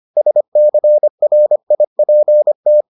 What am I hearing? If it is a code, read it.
Morse code, heard as a keyed tone, about 25 wpm, 600 Hz. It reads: SCRIPT